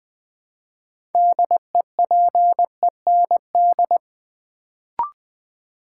Morse code, heard as a keyed tone, about 20 wpm, 700 Hz. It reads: DEPEND
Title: Build A Word - 3 Letters At A Time - 6-Letter Words - 20wpm